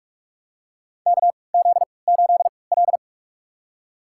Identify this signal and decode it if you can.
Morse code, heard as a keyed tone, about 44 wpm, spaced as for 25 wpm, 690 Hz. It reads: KZ8L
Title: USA Call Signs - 4 Characters - 3x Character Spacing - 45wpm